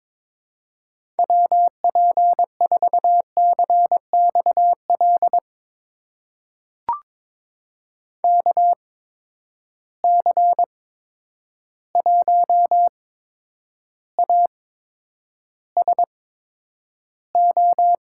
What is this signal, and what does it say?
Morse code, heard as a keyed tone, about 22 wpm, spaced as for 9 wpm, 700 Hz. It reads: WP4CXL KC1ASO
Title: USA Call Signs - 6 Characters - 8x Character Spacing - 22wpm